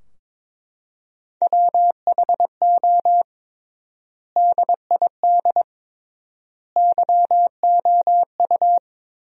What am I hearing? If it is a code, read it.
Morse code, heard as a keyed tone, about 22 wpm, 700 Hz. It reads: WHO DID YOU
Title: 3x Word-Spacing - Sentences from Top 300 Words - 22wpm